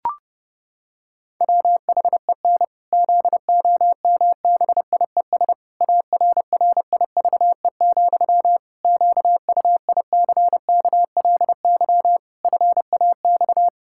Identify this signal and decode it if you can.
Morse code, heard as a keyed tone, about 30 wpm, 700 Hz. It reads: WHEN ZOMBIES ARRIVE, QUICKLY FAX
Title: Pangram Challenge v2 - 30wpm